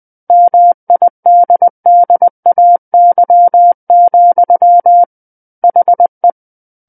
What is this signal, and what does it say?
Morse code, heard as a keyed tone, about 20 wpm, 700 Hz. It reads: MIDDAY, HE
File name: 20 WPM